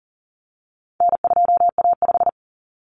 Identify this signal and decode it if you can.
Morse code, heard as a keyed tone, about 40 wpm, 700 Hz. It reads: N2A5
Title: Common Cut Numbers - 4-Digit 40wpm